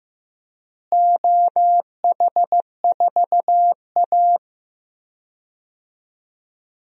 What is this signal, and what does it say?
Morse code, heard as a keyed tone, about 15 wpm, 700 Hz. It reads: OH4A
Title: DXCC-100 Call Signs - 4 Characters - Repeated 3x - 15wpm